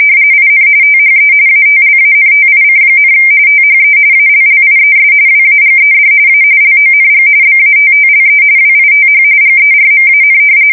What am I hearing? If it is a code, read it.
RTTY, about 45.45 baud. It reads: IN THE ATLANTIC, THE ALLIES WAGED A LONG BATTLE WITH GERMAN
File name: fsk170 / usnavy4